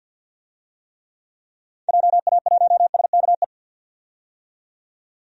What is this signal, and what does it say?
Morse code, heard as a keyed tone, about 46 wpm, 700 Hz. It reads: JA1SKE
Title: DXCC-100 Call Signs - 6 Characters - 50wpm